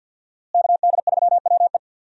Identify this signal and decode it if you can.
Morse code, heard as a keyed tone, about 46 wpm, 700 Hz. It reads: KD3WE
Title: USA Call Signs - 5 Characters - Repeated 3x - 50wpm